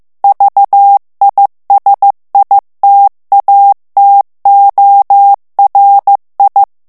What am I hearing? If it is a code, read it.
Morse code, heard as a keyed tone, about 15 wpm, 800 Hz. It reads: VISITATORI
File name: ITALIEN texte clair 1 vitesse 14 mots minute